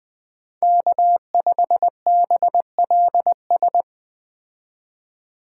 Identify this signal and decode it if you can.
Morse code, heard as a keyed tone, about 21 wpm, 700 Hz. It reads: K5BLS